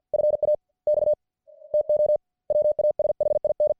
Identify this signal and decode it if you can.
Morse code, heard as a keyed tone, about 50 wpm, spaced as for 42 wpm, 600 Hz. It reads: FAX TO WASHIN